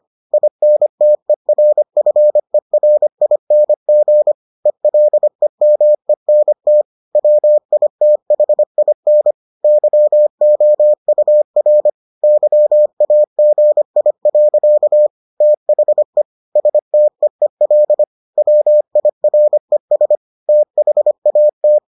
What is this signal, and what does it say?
Morse code, heard as a keyed tone, about 25 wpm, 600 Hz. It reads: INTERFERING ELEMENT WITHIN YOUR YAGI. THE STEEL WIRES THAT